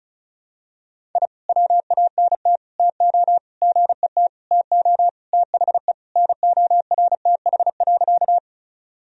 Morse code, heard as a keyed tone, about 35 wpm, 700 Hz. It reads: I WANT TO GET TO THE NORTH.
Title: Sentences from Top 1000 Words - Review Part 1 - 35wpm